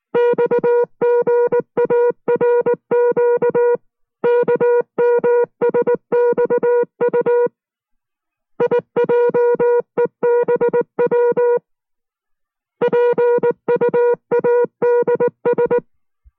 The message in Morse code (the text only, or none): XGARQ KMSXU IJEBW PUADS